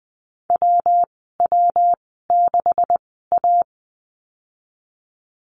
Morse code, heard as a keyed tone, about 19 wpm, spaced as for 14 wpm, 700 Hz. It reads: WW6A
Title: 161 Lesson 26g - Letter g - Call Signs 20wpm